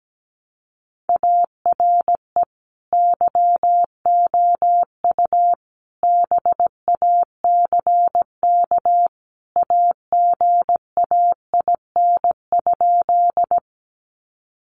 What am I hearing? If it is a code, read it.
Morse code, heard as a keyed tone, about 17 wpm, 700 Hz. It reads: ARE YOU BACK AGAIN?
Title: Sentences from Top 200 Words 17wpm